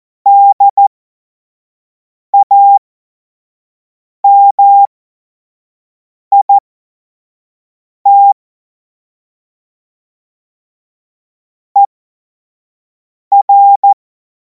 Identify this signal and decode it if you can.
Morse code, heard as a keyed tone, about 14 wpm, spaced as for 5 wpm, 800 Hz. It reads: DAMIT ER
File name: Morsekurs / Witze / Witz 05. Mai 2026 25 F70